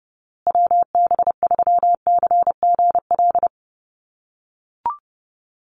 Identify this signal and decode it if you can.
Morse code, heard as a keyed tone, about 30 wpm, 700 Hz. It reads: WB3CGL
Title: USA Call Signs Repeated 2x 30wpm